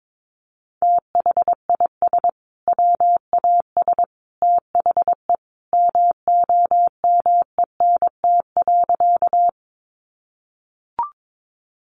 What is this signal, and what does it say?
Morse code, heard as a keyed tone, about 22 wpm, 700 Hz. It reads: THIS WAS THE MOMENT.
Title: Sentences from Top 1000 Words 22wpm